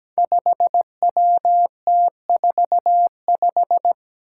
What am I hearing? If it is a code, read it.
Morse code, heard as a keyed tone, about 17 wpm, 700 Hz. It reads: 5WT45
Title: Five Challenging Letter-Number Combinations 17wpm